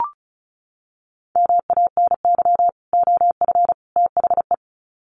Morse code, heard as a keyed tone, about 34 wpm, 700 Hz. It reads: MANY OF THE
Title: Sets of 3 Words 35wpm